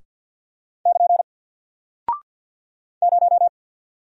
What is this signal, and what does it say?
Morse code, heard as a keyed tone, about 50 wpm, 700 Hz. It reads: <KN> 0
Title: Single Character (Letter:Number:Punctuation:Prosign) Alt 50wpm